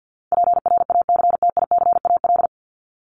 Morse code, heard as a keyed tone, about 50 wpm, 700 Hz. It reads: PRACTICAL
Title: LY Suffix - Top 100 Words with Base Words 50wpm